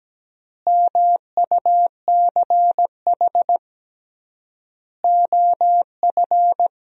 Morse code, heard as a keyed tone, about 17 wpm, 700 Hz. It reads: MUCH OF
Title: 3x Word-Spacing - Sentences from Top 200 Words - 17wpm